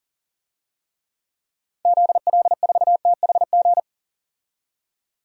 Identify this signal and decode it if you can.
Morse code, heard as a keyed tone, about 40 wpm, 700 Hz. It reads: ZL4THG
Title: DXCC-100 Call Signs - 6 Characters - 40wpm